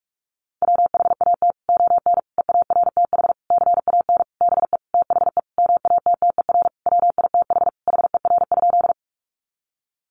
Morse code, heard as a keyed tone, about 44 wpm, 700 Hz. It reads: WHAT ON EARTH CAN BE THE MATTER WITH HER?
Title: Sentences from Top 700 Words 45wpm